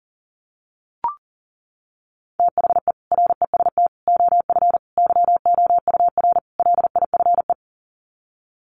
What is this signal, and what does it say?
Morse code, heard as a keyed tone, about 40 wpm, 700 Hz. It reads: THE REST OF YOUR LIFE
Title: Sets of 5 Words 40wpm